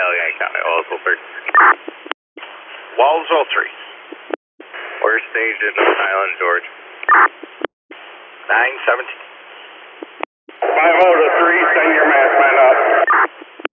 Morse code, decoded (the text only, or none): none